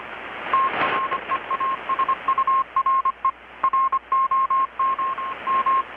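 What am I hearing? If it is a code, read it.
Morse code, heard as a keyed tone, about 25 wpm, 1070 Hz, with radio noise. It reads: TREASURE ROM M
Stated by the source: Morse from KPH's Night of Nights XX maritime radiotelegraphy event, received in Chiapas, Mexico about 3,800 km away on a Kenwood R-600 with a grounded 10 m long-wire antenna.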